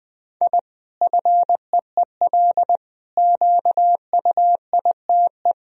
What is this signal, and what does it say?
Morse code, heard as a keyed tone, about 21 wpm, 700 Hz. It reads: I FEEL QUITE
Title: Sentences from Top 1000 Words - Review Part 3 - 20wpm